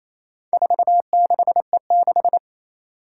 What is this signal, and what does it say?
Morse code, heard as a keyed tone, about 27 wpm, 700 Hz. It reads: 46E6
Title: Four Challenging Letter-Number Combinations 28wpm